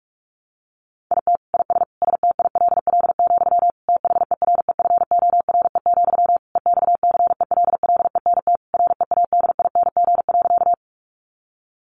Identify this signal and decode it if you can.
Morse code, heard as a keyed tone, about 44 wpm, 700 Hz. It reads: IT IS STILL, THEREFORE, EXCELLENT READING.
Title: Sentences from Top 1000 Words 45wpm